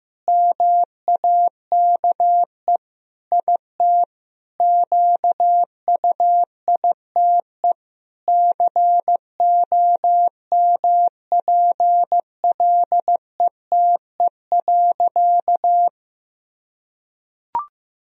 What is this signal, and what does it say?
Morse code, heard as a keyed tone, about 16 wpm, 700 Hz. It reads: MAKE IT QUITE COMPLETE.
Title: Sentences from Top 1000 Words - Review Part 2 - 15wpm